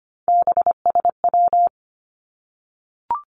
Morse code, heard as a keyed tone, about 25 wpm, 700 Hz. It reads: BSW
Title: Three Challenging Letter-Number Combinations 25wpm